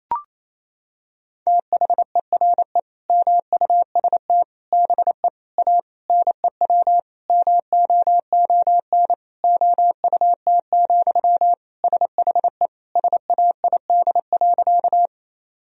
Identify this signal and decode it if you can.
Morse code, heard as a keyed tone, about 28 wpm, 690 Hz. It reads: THERE MUST BE A NEW MOON OUT, SHE SAID.
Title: Sentences from Top 1000 Words - Review Part 3 - 28wpm